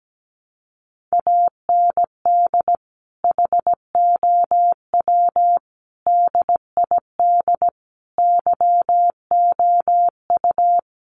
Morse code, heard as a keyed tone, about 17 wpm, 700 Hz. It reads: AND HOW DID YOU
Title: Sentences from Top 1200 Words 17wpm